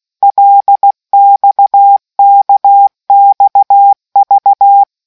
Morse code, heard as a keyed tone, about 16 wpm, 800 Hz. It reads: LXKXV